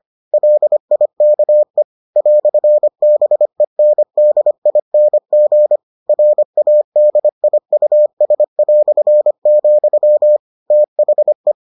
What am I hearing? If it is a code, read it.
Morse code, heard as a keyed tone, about 25 wpm, 600 Hz. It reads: LIKE "BENDING RADIUS", THE